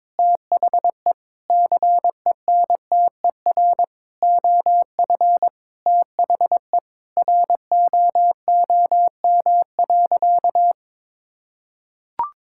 Morse code, heard as a keyed tone, about 23 wpm, 700 Hz. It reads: THE CENTER OF THE ROOM.
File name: practice / 22wpm / Sentences with 4-Word Phrases - 22wpm